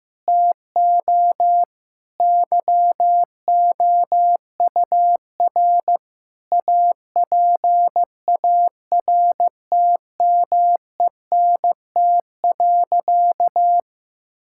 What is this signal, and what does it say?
Morse code, heard as a keyed tone, about 15 wpm, 700 Hz. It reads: TO YOUR APARTMENT.
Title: Sentences from Top 1000 Words - Review Part 6 - 15wpm